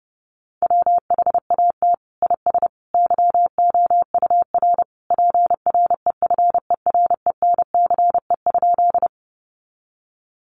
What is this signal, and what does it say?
Morse code, heard as a keyed tone, about 30 wpm, 700 Hz. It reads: WHAT IS YOUR PREFERENCE?
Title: Sentences from Top 1500 Words 30wpm